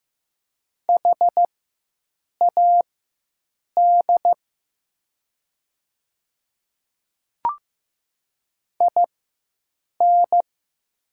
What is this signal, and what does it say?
Morse code, heard as a keyed tone, about 15 wpm, spaced as for 7 wpm, 700 Hz. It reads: HAD IN
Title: 4x Character Spacing - Top 100 Words - 15wpm